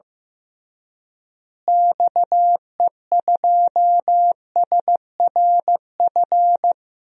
Morse code, heard as a keyed tone, about 15 wpm, 700 Hz. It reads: XE2SRF
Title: DXCC-100 Call Signs - 6 Characters - Repeated 3x - 15wpm